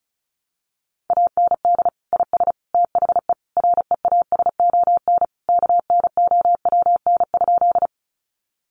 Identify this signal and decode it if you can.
Morse code, heard as a keyed tone, about 34 wpm, 700 Hz. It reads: AND IS THE REASON KNOWN?
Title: Sentences from Top 700 Words 35wpm